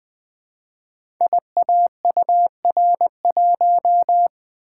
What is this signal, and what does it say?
Morse code, heard as a keyed tone, about 20 wpm, 700 Hz. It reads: IAUR1
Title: Code Groups - 5 Letter:Numbers - 20wpm